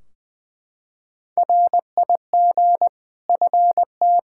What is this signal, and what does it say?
Morse code, heard as a keyed tone, about 20 wpm, 700 Hz. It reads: RIG FT
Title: K1USN SST QSO Practice 20wpm